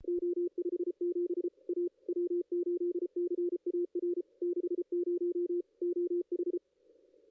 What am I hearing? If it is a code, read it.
Morse code, heard as a keyed tone, about 33 wpm, 350 Hz, with radio noise. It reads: O57 A W8CAR 60 OH
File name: contests / ss-cw-k5tr-op-n5rz -2300